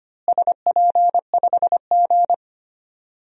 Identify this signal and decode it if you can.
Morse code, heard as a keyed tone, about 25 wpm, 700 Hz. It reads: SP5G
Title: DXCC-100 Call Signs - 4 Characters - Repeated 3x - 25wpm